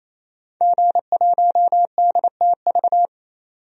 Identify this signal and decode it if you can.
Morse code, heard as a keyed tone, about 28 wpm, 700 Hz. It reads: G1DTV
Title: DXCC-100 Call Signs - 5 Characters - 28wpm